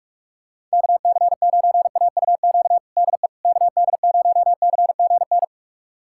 Morse code, heard as a keyed tone, about 46 wpm, 700 Hz. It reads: KC9AUQ DE KD0CGN
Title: USA Callsign DE Callsign - 6 Characters - 45wpm